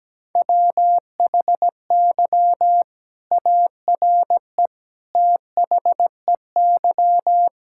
Morse code, heard as a keyed tone, about 17 wpm, 700 Hz. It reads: WHY ARE THEY
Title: Sentences from Top 1000 Words - Review Part 6 - 17wpm